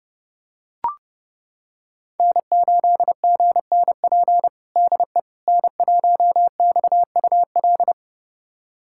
Morse code, heard as a keyed tone, about 32 wpm, 700 Hz. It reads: N8GNP DE N1XUL
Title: USA Callsign DE Callsign - 5 Characters - 30wpm